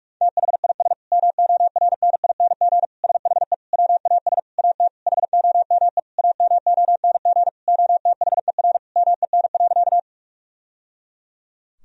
Morse code, heard as a keyed tone, about 46 wpm, 700 Hz. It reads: THIS MORNING SHE WAS AT HOME AMONG OTHER MEN.
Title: Sentences from Top 1000 Words - Review Part 3 - 45wpm